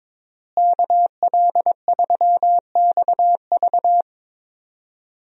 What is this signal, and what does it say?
Morse code, heard as a keyed tone, about 22 wpm, 700 Hz. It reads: KL3XV
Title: USA Call Signs - 5 Characters - Repeated 3x - 22wpm